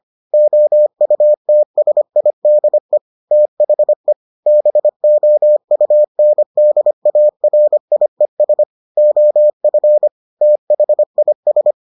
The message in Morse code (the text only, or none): OUTSIDE THE BOUNDARIES OF THIS